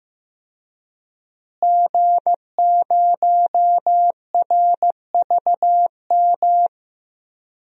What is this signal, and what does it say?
Morse code, heard as a keyed tone, about 16 wpm, 700 Hz. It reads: G0RVM
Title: DXCC-100 Call Signs - 5 Characters - Repeated 3x - 15wpm